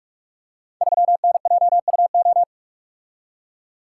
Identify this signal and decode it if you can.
Morse code, heard as a keyed tone, about 47 wpm, 700 Hz. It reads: 3NJUO
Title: Code Groups - 5 Letter:Numbers - 45wpm